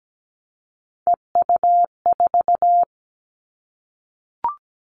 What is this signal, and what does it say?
Morse code, heard as a keyed tone, about 17 wpm, 700 Hz. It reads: EU4
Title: Callsign Prefixes for Top 50 DXCC - 17wpm